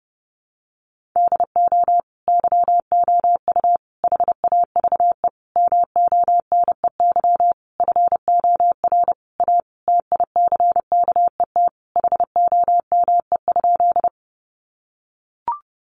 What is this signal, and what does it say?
Morse code, heard as a keyed tone, about 30 wpm, 700 Hz. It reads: DO YOU HAVE MONEY FOR A TICKET HOME?
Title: Sentences from Top 1400 Words 30wpm